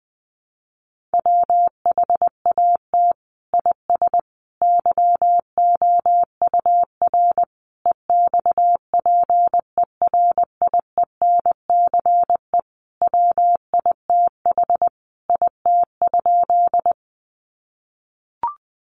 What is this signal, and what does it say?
Morse code, heard as a keyed tone, about 20 wpm, 700 Hz. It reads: WHAT IS YOUR EXPERIENCE WITH IT?
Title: Sentences from Top 600 Words 20wpm